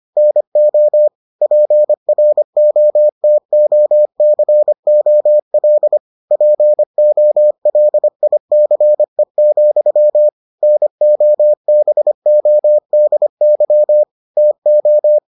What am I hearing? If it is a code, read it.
Morse code, heard as a keyed tone, about 25 wpm, 600 Hz. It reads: NO PROTOCOL POLICE, NOBODY TO